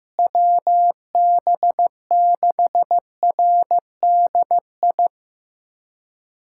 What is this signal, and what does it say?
Morse code, heard as a keyed tone, about 16 wpm, 700 Hz. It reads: WB6RDI
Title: USA Callsign DE Callsign - 6 Characters - 15wpm